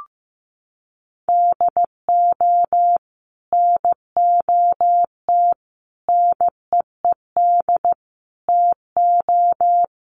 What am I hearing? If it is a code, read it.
Morse code, heard as a keyed tone, about 15 wpm, 700 Hz. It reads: DO NOT NEED TO
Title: Sets of 4 Words 15wpm